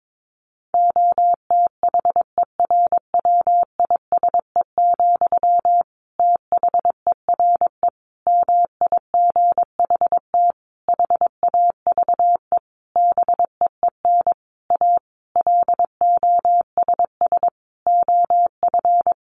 Morse code, heard as a keyed tone, about 22 wpm, 700 Hz. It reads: OTHERWISE, THERE MIGHT HAVE BEEN A LOSS OF
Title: Sentences from Top 1400 Words 22wpm